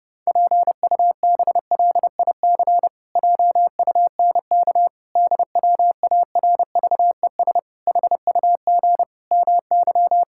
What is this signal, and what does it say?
Morse code, heard as a keyed tone, about 32 wpm, 700 Hz. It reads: PUBLIC JUNK DWARVES HUG MY
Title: Pangram Challenge v2 - 30wpm